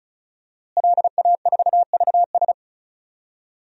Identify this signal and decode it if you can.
Morse code, heard as a keyed tone, about 36 wpm, 700 Hz. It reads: LA4VS
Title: DXCC-100 Call Signs - 5 Characters - Repeated 2x - 35wpm